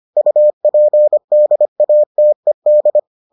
Morse code, heard as a keyed tone, about 25 wpm, 600 Hz. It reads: UPDATED